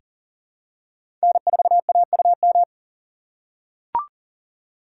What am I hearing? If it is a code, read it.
Morse code, heard as a keyed tone, about 40 wpm, 700 Hz. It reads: N4AUM
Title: USA Call Signs - 5 Characters - 4x Character Spacing - 40wpm